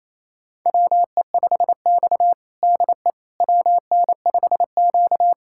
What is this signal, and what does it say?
Morse code, heard as a keyed tone, about 28 wpm, 700 Hz. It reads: WE5X DE WN5Q
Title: USA Callsign DE Callsign - 4 Characters - 28wpm